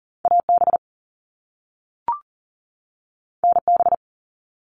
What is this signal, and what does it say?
Morse code, heard as a keyed tone, about 40 wpm, 700 Hz. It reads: AB NB
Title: Canadian Provinces and Territories 40wpm